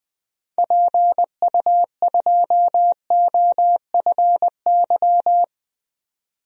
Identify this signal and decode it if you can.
Morse code, heard as a keyed tone, about 20 wpm, 700 Hz. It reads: PU2OFY
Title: DXCC-100 Call Signs - 6 Characters - 20wpm